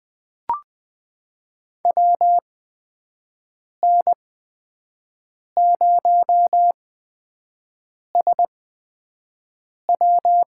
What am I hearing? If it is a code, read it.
Morse code, heard as a keyed tone, about 20 wpm, spaced as for 5 wpm, 700 Hz. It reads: WN0SW
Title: USA Call Signs - 5 Characters - 8x Character Spacing - 20wpm